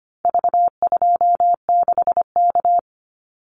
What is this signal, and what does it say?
Morse code, heard as a keyed tone, about 25 wpm, 700 Hz. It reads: V26K
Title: Call Signs Repeated 2x - 25wpm